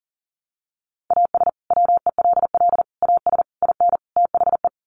Morse code, heard as a keyed tone, about 37 wpm, 700 Hz. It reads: AS WELL AS IN THE
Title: Sets of 5 Words Encore 40wpm